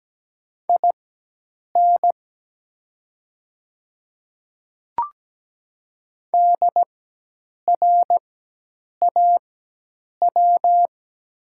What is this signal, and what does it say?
Morse code, heard as a keyed tone, about 17 wpm, spaced as for 8 wpm, 700 Hz. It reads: IN DRAW
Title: 4x Character Spacing - Top 300 Words - 17wpm